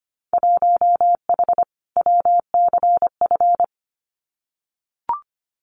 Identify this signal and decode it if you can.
Morse code, heard as a keyed tone, about 25 wpm, 690 Hz. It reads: 1H WCF
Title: Winter Field Day Exchanges 25wpm